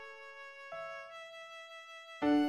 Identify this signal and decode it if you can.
There is no code.